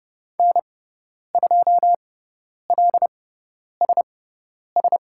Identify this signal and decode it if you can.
Morse code, heard as a keyed tone, about 30 wpm, spaced as for 10 wpm, 700 Hz. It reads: N2LSS
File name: Lesson 17g - Letter u - Call Signs 30wpm